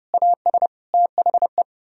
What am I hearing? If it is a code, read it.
Morse code, heard as a keyed tone, about 40 wpm, 700 Hz. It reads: AS THE